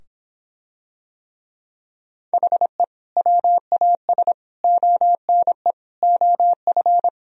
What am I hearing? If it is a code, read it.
Morse code, heard as a keyed tone, about 26 wpm, 700 Hz. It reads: HE WAS ONE OF